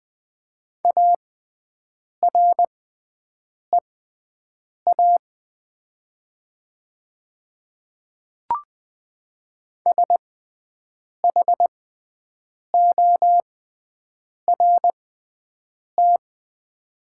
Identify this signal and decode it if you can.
Morse code, heard as a keyed tone, about 20 wpm, spaced as for 7 wpm, 700 Hz. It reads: AREA SHORT
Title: 6x Character Spacing - Top 500 Words - 20wpm